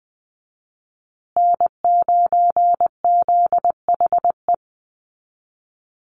Morse code, heard as a keyed tone, about 20 wpm, 700 Hz. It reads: N9ZHE
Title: USA Call Signs Repeated 2x 20wpm